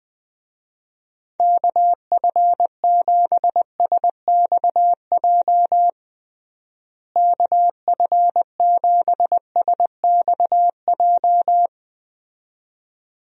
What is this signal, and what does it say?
Morse code, heard as a keyed tone, about 20 wpm, 700 Hz. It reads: KF7SXJ KF7SXJ
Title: USA Call Signs - 6 Characters - Repeated 2x - 20wpm